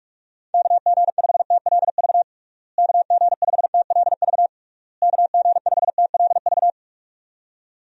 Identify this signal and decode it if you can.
Morse code, heard as a keyed tone, about 46 wpm, 700 Hz. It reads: KG5TLV KG5TLV KG5TLV
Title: USA Call Signs - 6 Characters - Repeated 3x - 45wpm